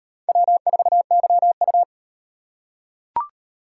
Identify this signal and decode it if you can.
Morse code, heard as a keyed tone, about 38 wpm, 700 Hz. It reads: W4YU